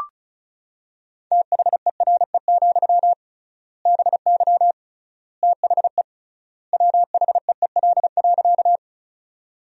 Morse code, heard as a keyed tone, about 36 wpm, 700 Hz. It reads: THERE, BY THE WHEEL.
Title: 3x Word-Spacing - Sentences from Top 500 Words - 35wpm